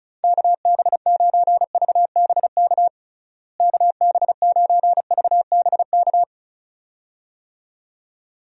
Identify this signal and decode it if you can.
Morse code, heard as a keyed tone, about 36 wpm, 700 Hz. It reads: KB9VBK KB9VBK
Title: USA Call Signs - 6 Characters - Repeated 2x - 35wpm